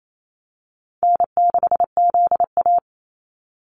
Morse code, heard as a keyed tone, about 28 wpm, 700 Hz. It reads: N6ZA
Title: NA SOTA Activator Callsigns 28wpm